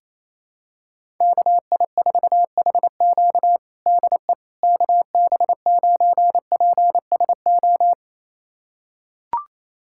Morse code, heard as a keyed tone, about 28 wpm, 700 Hz. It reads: KI4HQ DE KB9PSO
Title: USA Callsign DE Callsign 28wpm